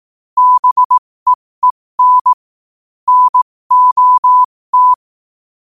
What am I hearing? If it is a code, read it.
Morse code, heard as a keyed tone, about 18 wpm, spaced as for 15 wpm, 1000 Hz. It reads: BEEN NOT
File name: qotdfiles / 15wpm